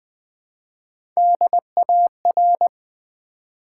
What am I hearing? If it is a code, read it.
Morse code, heard as a keyed tone, about 20 wpm, 700 Hz. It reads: DAR
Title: Build A Word - 1 Letter At A Time - 4-Letter Words - 20wpm